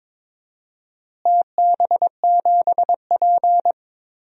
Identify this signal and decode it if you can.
Morse code, heard as a keyed tone, about 23 wpm, 700 Hz. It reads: TB7P
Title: DXCC-100 Call Signs - 4 Characters - Repeated 3x - 22wpm